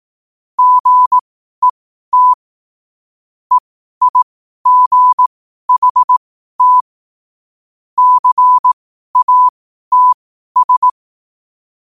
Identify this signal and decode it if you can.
Morse code, heard as a keyed tone, about 18 wpm, spaced as for 12 wpm, 1000 Hz. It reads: GET EIGHT CATS
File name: qotdfiles / 12wpm